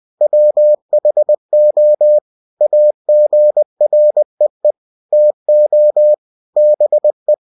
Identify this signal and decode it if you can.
Morse code, heard as a keyed tone, about 20 wpm, 600 Hz. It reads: WHO AGREE TO BE
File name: cw-books / winnie-the-pooh / winnie-20wpm-page-0113